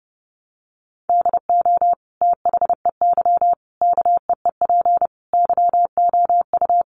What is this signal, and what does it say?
Morse code, heard as a keyed tone, about 30 wpm, 700 Hz. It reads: DO THEY KEEP YOU
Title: Sentences from Top 1200 Words 30wpm